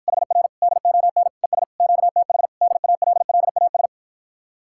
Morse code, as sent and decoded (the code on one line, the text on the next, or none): .... .-. / -.. --- -. / . ... / --.- - .... / -.. .- .-.. .-.. .- ...
HR DON ES QTH DALLAS